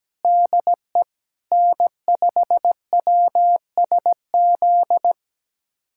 Morse code, heard as a keyed tone, about 17 wpm, 700 Hz. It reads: DE N5WSZ
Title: USA Callsign DE Callsign - 5 Characters - 17wpm